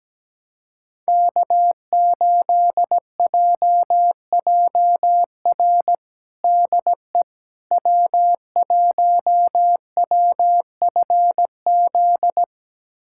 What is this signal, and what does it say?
Morse code, heard as a keyed tone, about 17 wpm, 700 Hz. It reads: K8JJR DE W1WFZ
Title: USA Callsign DE Callsign - 5 Characters - 17wpm